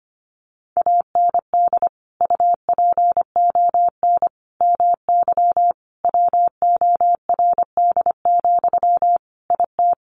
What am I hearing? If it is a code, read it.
Morse code, heard as a keyed tone, about 25 wpm, 700 Hz. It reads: AND UPON MY WORD, IT
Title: Sentences from Top 1100 Words 25wpm